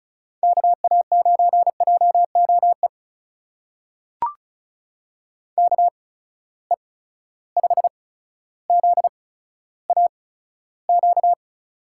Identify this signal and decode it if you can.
Morse code, heard as a keyed tone, about 36 wpm, spaced as for 15 wpm, 700 Hz. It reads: KA9JOE KE5ZAQ
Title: USA Call Signs - 6 Characters - 8x Character Spacing - 35wpm